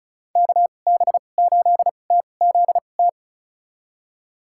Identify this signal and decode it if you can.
Morse code, heard as a keyed tone, about 34 wpm, spaced as for 25 wpm, 700 Hz. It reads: KB8TZT
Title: USA Call Signs - 6 Characters - 2x Character Spacing - 35wpm